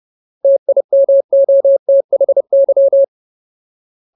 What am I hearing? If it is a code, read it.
Morse code, heard as a keyed tone, about 30 wpm, 550 Hz. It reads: TIMOTHY
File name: Ultimate Word List / 30WPM Ultimate